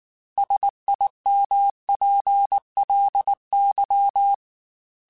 Morse code, heard as a keyed tone, about 19 wpm, 800 Hz. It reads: SIMPLY
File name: Technical Subjects / Technical Subjects-mp3s / Sw Pwr Sup @18